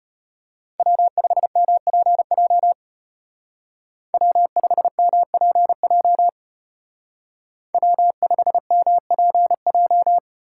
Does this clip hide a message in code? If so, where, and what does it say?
Yes: Morse code from 0.8 to 10.2 s, heard as a keyed tone, about 33 wpm, 700 Hz. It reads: W5MPJ W5MPJ W5MPJ